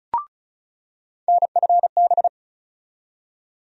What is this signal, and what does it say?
Morse code, heard as a keyed tone, about 36 wpm, 700 Hz. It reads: NFB
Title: USA Call Sign Postfixes - 3 Letters 35wpm